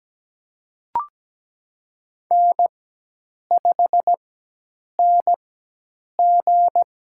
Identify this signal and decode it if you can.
Morse code, heard as a keyed tone, about 17 wpm, spaced as for 8 wpm, 700 Hz. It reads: N5NG
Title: USA Call Signs - 4 Characters - 4x Character Spacing - 17wpm